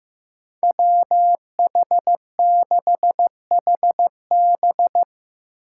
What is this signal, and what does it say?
Morse code, heard as a keyed tone, about 15 wpm, 700 Hz. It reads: WH6HB
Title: USA Call Signs - 5 Characters - 4x Character Spacing - 15wpm